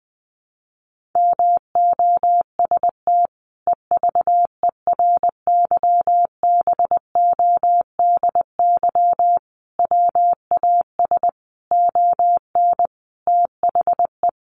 Morse code, heard as a keyed tone, about 20 wpm, 700 Hz. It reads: MOST EVERYBODY WAS ON THE
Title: Sentences from Top 1400 Words 20wpm